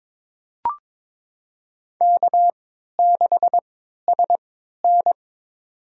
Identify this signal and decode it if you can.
Morse code, heard as a keyed tone, about 22 wpm, spaced as for 12 wpm, 700 Hz. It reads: K6SN
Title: USA Call Signs - 4 Characters - 3x Character Spacing - 22wpm